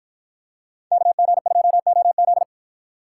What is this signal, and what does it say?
Morse code, heard as a keyed tone, about 49 wpm, 700 Hz. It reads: KG2OZ